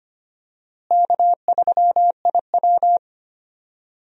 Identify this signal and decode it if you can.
Morse code, heard as a keyed tone, about 25 wpm, 700 Hz. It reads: K3IW
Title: USA Call Signs - 4 Characters - Repeated 3x - 25wpm